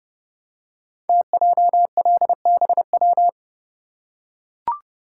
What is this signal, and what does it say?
Morse code, heard as a keyed tone, about 30 wpm, 700 Hz. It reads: TJLBW
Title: Code Groups - 5 Letter:Numbers - 30wpm